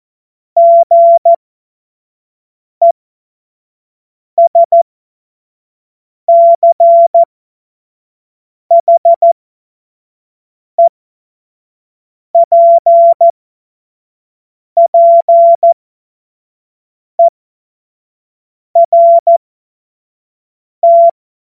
Morse code, heard as a keyed tone, about 14 wpm, spaced as for 5 wpm, 680 Hz. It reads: GESCHEPPERT
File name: Witz-05-August-2025-25-F70